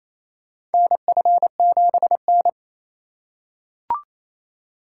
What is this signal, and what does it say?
Morse code, heard as a keyed tone, about 28 wpm, 700 Hz. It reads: NF7N
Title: USA Call Signs - 4 Characters - 3x Character Spacing - 28wpm